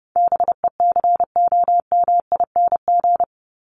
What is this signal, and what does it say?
Morse code, heard as a keyed tone, about 30 wpm, 700 Hz. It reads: BECOMING
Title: ING Suffix - Top 100 Words with Base Words 30wpm